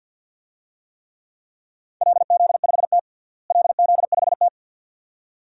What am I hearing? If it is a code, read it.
Morse code, heard as a keyed tone, about 47 wpm, 700 Hz. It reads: LZ5T LZ5T
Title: DXCC-100 Call Signs - 4 Characters - Repeated 2x - 50wpm